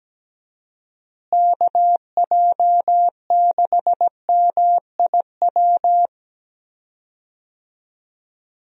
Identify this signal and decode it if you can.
Morse code, heard as a keyed tone, about 17 wpm, 700 Hz. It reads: KJ6MIW
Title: USA Call Signs - 6 Characters - Repeated 2x - 17wpm